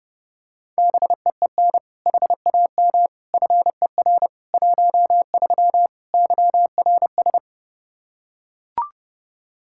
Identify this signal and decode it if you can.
Morse code, heard as a keyed tone, about 31 wpm, 700 Hz. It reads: BEEN HAM FER 13 YRS
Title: Practice QSO - Exchange 3 - More Info - 30wpm